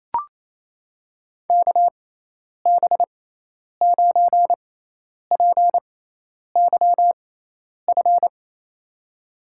Morse code, heard as a keyed tone, about 28 wpm, spaced as for 10 wpm, 700 Hz. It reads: KB9PYF
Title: USA Call Signs - 6 Characters - 6x Character Spacing - 28wpm